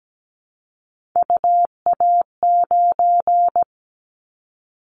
Morse code, heard as a keyed tone, about 17 wpm, 700 Hz. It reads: UA9
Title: Callsign Prefixes for Top 50 DXCC - 17wpm